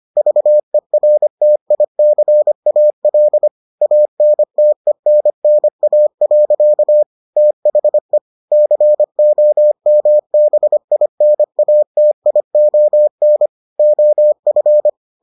Morse code, heard as a keyed tone, about 25 wpm, 600 Hz. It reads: VERTICAL ANTENNA. THE COMBINATION OF